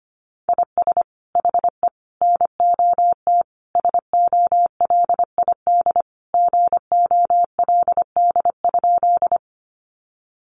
Morse code, heard as a keyed tone, about 25 wpm, 700 Hz. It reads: IS HE NOT SOLID GOLD?
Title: Sentences from Top 1500 Words 25wpm